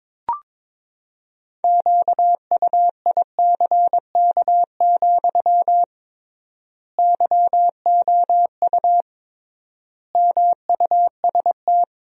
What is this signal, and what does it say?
Morse code, heard as a keyed tone, about 22 wpm, 700 Hz. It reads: QUICK, YOU MUST
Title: 3x Word-Spacing - Sentences from Top 500 Words - 22wpm